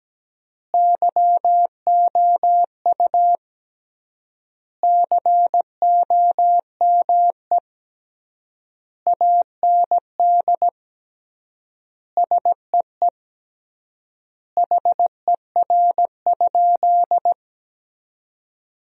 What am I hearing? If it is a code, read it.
Morse code, heard as a keyed tone, about 17 wpm, 700 Hz. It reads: YOU COME AND SEE HER?
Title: 3x Word-Spacing - Sentences from Top 100 Words - 17wpm